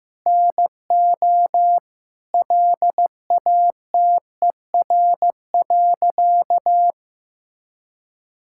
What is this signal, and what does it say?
Morse code, heard as a keyed tone, about 15 wpm, 700 Hz. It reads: NO LATER.
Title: Sentences from Top 1000 Words - Review Part 5 - 15wpm